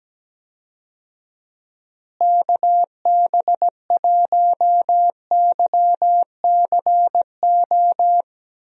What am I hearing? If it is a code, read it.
Morse code, heard as a keyed tone, about 17 wpm, 700 Hz. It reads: KB1YCO
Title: USA Call Signs - 6 Characters - Repeated 2x - 17wpm